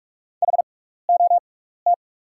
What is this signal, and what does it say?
Morse code, heard as a keyed tone, about 47 wpm, spaced as for 16 wpm, 690 Hz. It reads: HOT
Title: 6x Character Spacing - Top 500 Words - 45wpm